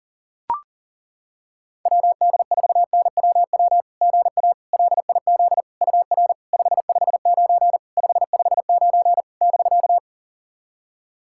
Morse code, heard as a keyed tone, about 37 wpm, 690 Hz. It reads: WD4NWW GA LIZ UR 559 559 <BK>
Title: SOTA QSO Practice 40wpm